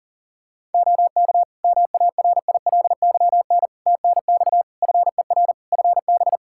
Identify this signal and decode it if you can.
Morse code, heard as a keyed tone, about 40 wpm, 700 Hz. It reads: OK MARILYN TNX FER FB
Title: Practice QSO - Exchange 4 - Ending - 40wpm